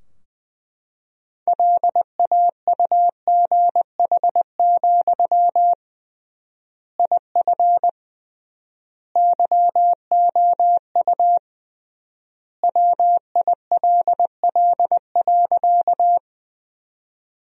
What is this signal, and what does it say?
Morse code, heard as a keyed tone, about 21 wpm, 700 Hz. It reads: LAUGH, IF YOU WILL.
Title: 3x Word-Spacing - Sentences from Top 500 Words - 20wpm